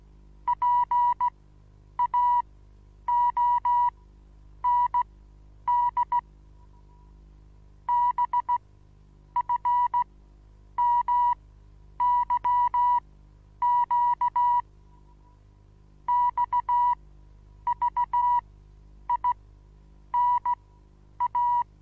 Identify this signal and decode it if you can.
Morse code, heard as a keyed tone, about 17 wpm, spaced as for 9 wpm, 970 Hz. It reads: PAOND BFMYQ XVINA
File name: Onlinelehrgang / morsen / DJ4UF Morsen25